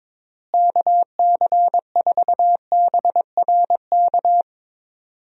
Morse code, heard as a keyed tone, about 22 wpm, 700 Hz. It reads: KC4BRK
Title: USA Call Signs - 6 Characters - Repeated 2x - 22wpm